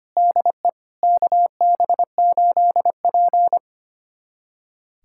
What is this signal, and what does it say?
Morse code, heard as a keyed tone, about 25 wpm, 700 Hz. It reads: DE KB8P
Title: USA Callsign DE Callsign - 4 Characters - 25wpm